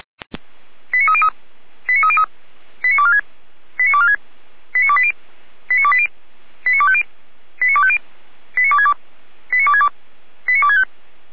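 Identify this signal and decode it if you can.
Five-tone selective call (ZVEI), heard as quick runs of five tones, tones of about 70 ms. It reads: 89303 89303 89246 89246 89280 89280 89250 89250 89262 89262 89265
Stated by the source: Five-tone ZVEI selective call received by SDR, from feuerwehr-orlamuende.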